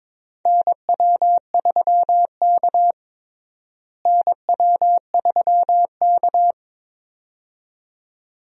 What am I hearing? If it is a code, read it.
Morse code, heard as a keyed tone, about 22 wpm, 700 Hz. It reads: NW3K NW3K
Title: USA Call Signs - 4 Characters - Repeated 2x - 22wpm